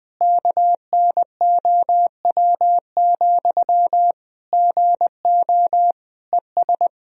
Morse code, heard as a keyed tone, about 20 wpm, 700 Hz. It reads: KNOW, GO ES
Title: Sentences from Top 200 Words and Abbreviations 20wpm